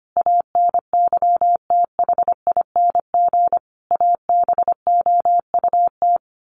Morse code, heard as a keyed tone, about 25 wpm, 700 Hz. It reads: ANYTHING ABOUT
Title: Sets of 5 Words 25wpm